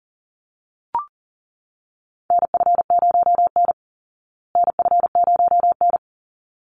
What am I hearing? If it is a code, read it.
Morse code, heard as a keyed tone, about 40 wpm, 700 Hz. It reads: NF0N NF0N
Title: Call Signs Repeated 3x - 40wpm